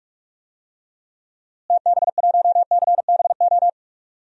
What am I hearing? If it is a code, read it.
Morse code, heard as a keyed tone, about 45 wpm, 700 Hz. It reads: TB1CBO
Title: DXCC-100 Call Signs - 6 Characters - Repeated 3x - 45wpm